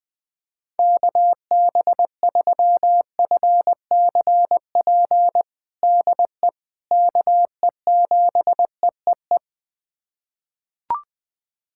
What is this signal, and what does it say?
Morse code, heard as a keyed tone, about 20 wpm, 700 Hz. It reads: KB3FCP DE KE7EEE
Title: USA Callsign DE Callsign 20wpm